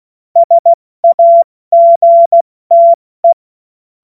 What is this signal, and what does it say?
Morse code, heard as a keyed tone, about 16 wpm, 680 Hz. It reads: SAGTE
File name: Morsekurs / Witze / Witz 16 Januar 2024 70 F80